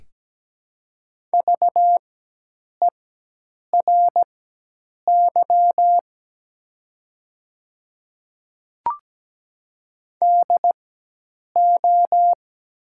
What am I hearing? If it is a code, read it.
Morse code, heard as a keyed tone, about 17 wpm, spaced as for 8 wpm, 700 Hz. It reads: VERY DO
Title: 4x Character Spacing - Top 200 Words - 17wpm